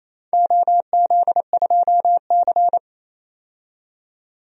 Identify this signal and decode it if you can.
Morse code, heard as a keyed tone, about 28 wpm, 700 Hz. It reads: OZ2C
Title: DXCC-100 Call Signs - 4 Characters - 28wpm